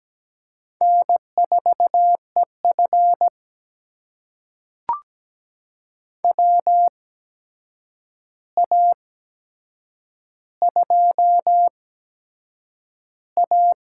Morse code, heard as a keyed tone, about 17 wpm, spaced as for 7 wpm, 700 Hz. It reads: N4EF WA2A
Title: USA Call Signs - 4 Characters - 8x Character Spacing - 17wpm